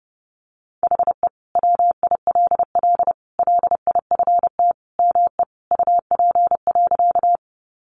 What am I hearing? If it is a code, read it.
Morse code, heard as a keyed tone, about 30 wpm, 700 Hz. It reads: HE WILL LIFT ME UP.
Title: Sentences from Top 1000 Words 30wpm